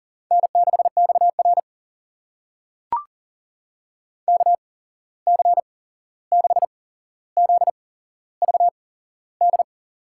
Morse code, heard as a keyed tone, about 40 wpm, spaced as for 15 wpm, 700 Hz. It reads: N6XR KC6ZVD
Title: USA Call Signs - 8x Character Spacing - 40wpm